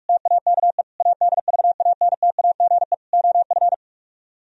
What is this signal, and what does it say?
Morse code, heard as a keyed tone, about 48 wpm, 700 Hz. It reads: TAKE ADVANTAGE OF